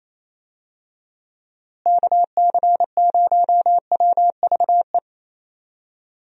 Morse code, heard as a keyed tone, about 28 wpm, 700 Hz. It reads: KC0WVE
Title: USA Call Signs - 6 Characters - Repeated 2x - 28wpm